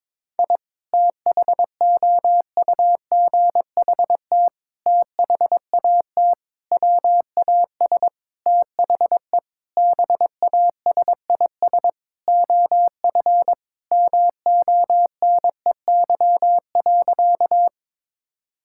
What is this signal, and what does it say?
Morse code, heard as a keyed tone, about 22 wpm, 700 Hz. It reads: I THOUGHT THAT WAS THE BASIS OF MONEY.
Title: Sentences from Top 1000 Words - Review Part 5 - 22wpm